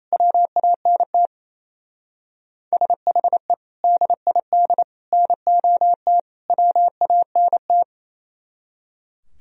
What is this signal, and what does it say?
Morse code, heard as a keyed tone, about 35 wpm, 700 Hz. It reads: WANT SHE DID NOT WANT
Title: Speed-Racing - Sets of 4 Words 28wpm